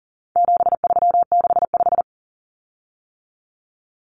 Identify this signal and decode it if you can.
Morse code, heard as a keyed tone, about 40 wpm, 700 Hz. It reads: 7365
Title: Common Cut Numbers - 4-Digit 40wpm